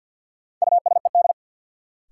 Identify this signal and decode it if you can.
Morse code, heard as a keyed tone, about 53 wpm, 700 Hz. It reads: USED
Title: Sets of 2 Words - New Word Drill - 50wpm